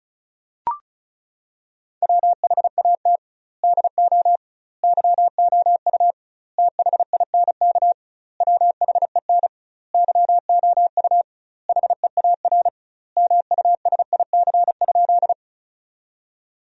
Morse code, heard as a keyed tone, about 36 wpm, 700 Hz. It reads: WHAT DO YOU THINK WHEN YOU HEAR MUSIC?
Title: 2x Word-Spacing - Sentences from Top 300 Words - 35wpm